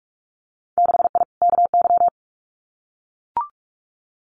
Morse code, heard as a keyed tone, about 44 wpm, 700 Hz. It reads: 6I KY
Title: Winter Field Day Exchanges 45wpm